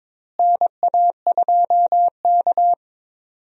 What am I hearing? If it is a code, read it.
Morse code, heard as a keyed tone, about 22 wpm, 700 Hz. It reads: NA2K
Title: USA Call Signs - 4 Characters - Repeated 2x - 22wpm